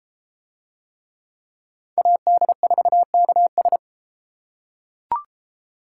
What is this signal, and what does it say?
Morse code, heard as a keyed tone, about 31 wpm, 700 Hz. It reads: AD4KS